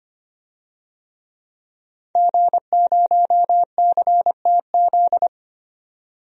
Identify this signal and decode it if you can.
Morse code, heard as a keyed tone, about 25 wpm, 700 Hz. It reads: G0CTZ
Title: DXCC-100 Call Signs - 5 Characters - 25wpm